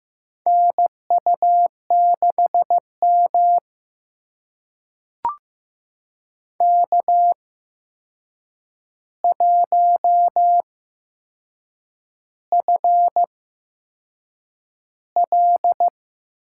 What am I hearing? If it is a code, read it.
Morse code, heard as a keyed tone, about 15 wpm, spaced as for 6 wpm, 700 Hz. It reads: NU6M K1FL
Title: USA Call Signs - 4 Characters - 8x Character Spacing - 15wpm